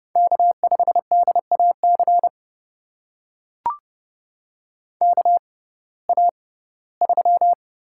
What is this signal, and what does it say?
Morse code, heard as a keyed tone, about 30 wpm, spaced as for 18 wpm, 700 Hz. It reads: K5DAC KA3
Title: USA Call Signs - 6x Character Spacing - 30wpm